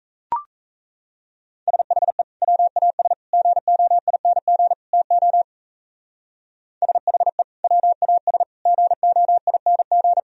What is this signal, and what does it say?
Morse code, heard as a keyed tone, about 37 wpm, 700 Hz. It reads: SHE WAS GOING TO SHE WAS GOING